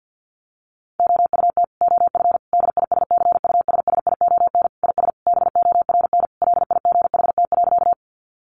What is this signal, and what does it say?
Morse code, heard as a keyed tone, about 50 wpm, 700 Hz. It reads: OUT OF DISCUSSION IS BORN LIGHT.
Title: Sentences from Top 1100 Words 50wpm